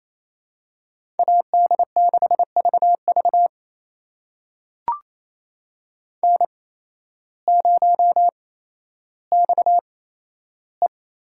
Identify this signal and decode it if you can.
Morse code, heard as a keyed tone, about 28 wpm, spaced as for 13 wpm, 700 Hz. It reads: AD6VV N0XE
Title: USA Call Signs - 5 Characters - 8x Character Spacing - 28wpm